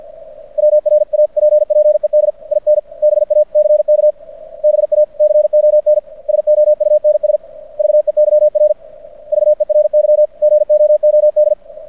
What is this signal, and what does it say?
Morse code, heard as a keyed tone, about 50 wpm, 600 Hz. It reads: WRAPPED IN BACK BACON SOUNS VEYR VERY GOOD
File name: archived / CRSnet